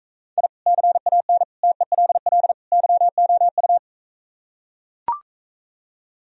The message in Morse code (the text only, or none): I CAN TELL YOU